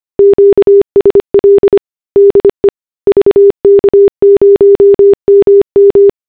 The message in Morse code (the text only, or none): QSL DE VK0MM